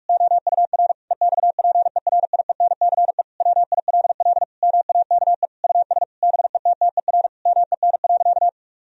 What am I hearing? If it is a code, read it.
Morse code, heard as a keyed tone, about 46 wpm, 700 Hz. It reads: OUR EXPERIENCE WILL MAKE US BETTER MEN.
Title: Sentences from Top 1000 Words - Review Part 3 - 45wpm